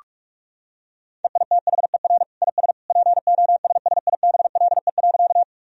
Morse code, heard as a keyed tone, about 46 wpm, 700 Hz. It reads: EITHER IS POSSIBLE.
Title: Sentences from Top 1000 Words - Review Part 4 - 45wpm